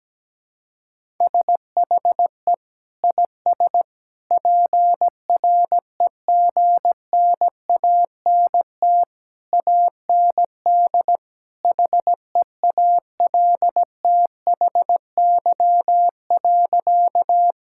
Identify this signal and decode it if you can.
Morse code, heard as a keyed tone, about 17 wpm, 700 Hz. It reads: SHE IS PREGNANT AND HEALTHY.
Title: Sentences from Top 1000 Words - Review Part 6 - 17wpm